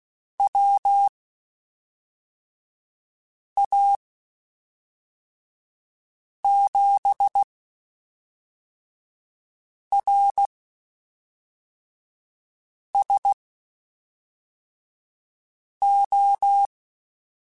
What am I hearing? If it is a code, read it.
Morse code, heard as a keyed tone, about 16 wpm, spaced as for 3 wpm, 800 Hz. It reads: WA7RSO
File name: (spaced) / QSOs Spaced-mp3s / WD0FBT 1 @5 sp